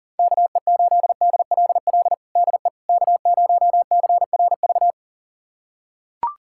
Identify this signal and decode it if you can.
Morse code, heard as a keyed tone, about 37 wpm, 700 Hz. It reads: KE8DLL DE K0CRV
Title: USA Callsign DE Callsign 40wpm